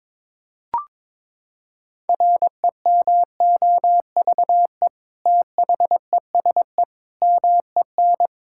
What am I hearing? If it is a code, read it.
Morse code, heard as a keyed tone, about 22 wpm, 700 Hz. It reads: REMOVE THESE MEN
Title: Sentences from Top 1000 Words - Review Part 5 - 22wpm